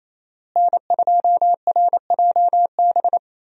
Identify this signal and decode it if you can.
Morse code, heard as a keyed tone, about 27 wpm, 700 Hz. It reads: N2RJB
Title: USA Call Signs - 5 Characters - Repeated 2x - 28wpm